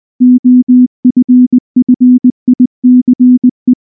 Morse code, heard as a keyed tone, about 19 wpm, 250 Hz. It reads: OFFICE
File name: morse / morse-training-20WPM-5